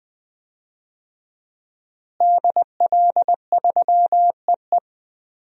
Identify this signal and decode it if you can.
Morse code, heard as a keyed tone, about 20 wpm, 700 Hz. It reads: DL3EE
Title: DXCC-100 Call Signs - 5 Characters - Repeated 3x - 20wpm